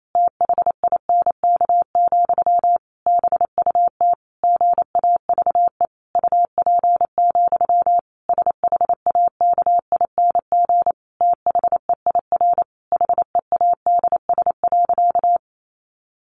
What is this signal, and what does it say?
Morse code, heard as a keyed tone, about 28 wpm, 700 Hz. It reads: THINK, BUT GAVE UP, SHAKING THEIR HEADS.
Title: A Sound Of Thunder 28wpm